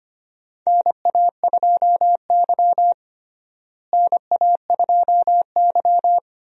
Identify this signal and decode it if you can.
Morse code, heard as a keyed tone, about 25 wpm, 700 Hz. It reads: NA2Y NA2Y
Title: USA Call Signs - 4 Characters - Repeated 3x - 25wpm